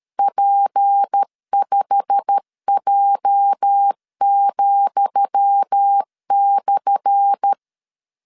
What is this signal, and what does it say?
Morse code, heard as a keyed tone, about 13 wpm, 800 Hz. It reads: P5J,/
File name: LEZIONI / 10 SIGNS / 10D COMA 12wpm 2544